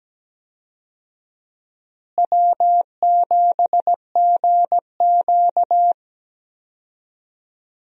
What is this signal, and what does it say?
Morse code, heard as a keyed tone, about 17 wpm, 700 Hz. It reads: W7GQ
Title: USA Call Signs - 4 Characters - Repeated 3x - 17wpm